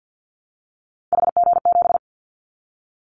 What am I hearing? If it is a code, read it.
Morse code, heard as a keyed tone, about 50 wpm, 700 Hz. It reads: HG7
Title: Callsign Prefixes for Top 25 DXCC - Repeated 3x - 50wpm